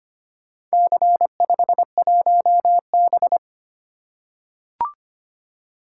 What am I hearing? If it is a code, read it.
Morse code, heard as a keyed tone, about 25 wpm, 690 Hz. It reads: C51B
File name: practice / 25wpm / Four Letter-Number Combinations - 25wpm